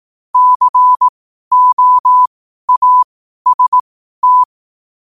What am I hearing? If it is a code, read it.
Morse code, heard as a keyed tone, about 18 wpm, spaced as for 12 wpm, 1000 Hz. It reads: COAST
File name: qotdfiles / 12wpm